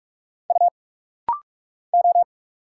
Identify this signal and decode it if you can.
Morse code, heard as a keyed tone, about 46 wpm, 690 Hz. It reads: U O
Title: Single Letters - Rapid-Fire 45wpm